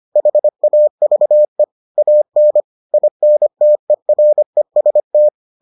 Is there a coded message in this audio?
Yes: Morse code from 0.2 to 5.3 s, heard as a keyed tone, about 25 wpm, 600 Hz. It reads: HAVE AN INTEREST